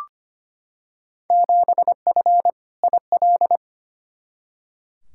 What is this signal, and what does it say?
Morse code, heard as a keyed tone, about 25 wpm, 700 Hz. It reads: 7F IL
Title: Field Day Exchanges Encore - 25wpm